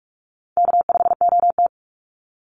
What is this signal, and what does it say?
Morse code, heard as a keyed tone, about 44 wpm, 690 Hz. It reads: K5OT
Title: NA SOTA Chaser Callsigns 45wpm